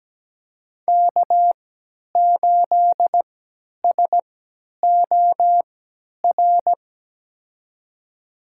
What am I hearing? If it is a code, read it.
Morse code, heard as a keyed tone, about 17 wpm, spaced as for 10 wpm, 700 Hz. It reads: K8SOR
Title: USA Call Signs - 3x Character Spacing - 17wpm